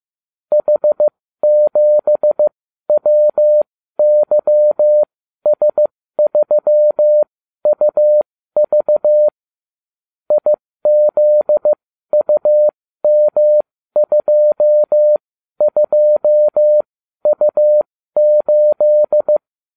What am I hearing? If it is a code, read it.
Morse code, heard as a keyed tone, about 15 wpm, spaced as for 12 wpm, 600 Hz. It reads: H7WYS3UV IZUM22U8